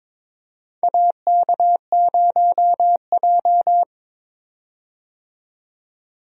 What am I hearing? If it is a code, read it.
Morse code, heard as a keyed tone, about 21 wpm, 700 Hz. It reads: AK0J
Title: USA Call Signs - 4 Characters - 22wpm